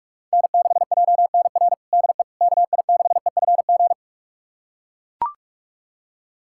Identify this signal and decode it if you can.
Morse code, heard as a keyed tone, about 42 wpm, 700 Hz. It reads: N6JNR DE KI6EFG
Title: USA Callsign DE Callsign 45wpm